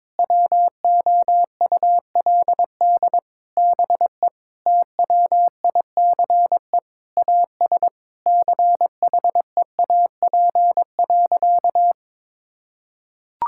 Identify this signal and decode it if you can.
Morse code, heard as a keyed tone, about 22 wpm, 700 Hz. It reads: WOULD BE TWICE AS CHEAP.
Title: Sentences from Top 1600 Words 22wpm